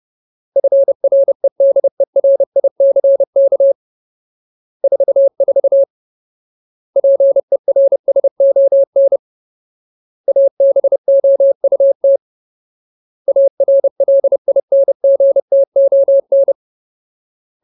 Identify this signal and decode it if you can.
Morse code, heard as a keyed tone, about 30 wpm, 550 Hz. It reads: FREDERICK 44 PERSON ABOUT ARLINGTON
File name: Ultimate Word List / 30WPM Ultimate